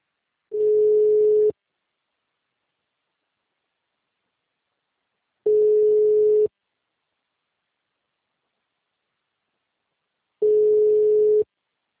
Ringback tone (Europe, 425 Hz).
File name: out- -2004- - - .150